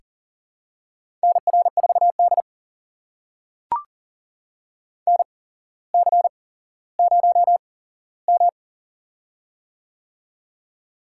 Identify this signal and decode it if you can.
Morse code, heard as a keyed tone, about 43 wpm, spaced as for 17 wpm, 700 Hz. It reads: NR4D NC0M
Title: USA Call Signs - 4 Characters - 8x Character Spacing - 40wpm